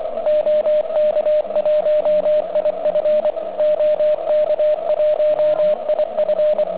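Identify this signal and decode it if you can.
Morse code, heard as a keyed tone, about 24 wpm, 590 Hz. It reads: OK1IF OK1IF
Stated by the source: Quick exchange between two contest stations on 3559 kHz, just after a QRP contest.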